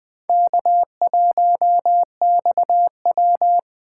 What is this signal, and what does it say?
Morse code, heard as a keyed tone, about 20 wpm, 700 Hz. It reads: K1XW
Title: Speed Racing - USA 1x2 Call Signs 17wpm